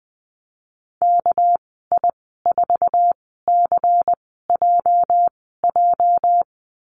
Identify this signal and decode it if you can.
Morse code, heard as a keyed tone, about 20 wpm, spaced as for 15 wpm, 700 Hz. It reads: KI4CJJ
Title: 219 Lesson 35g - Number 0 - Call Signs 20wpm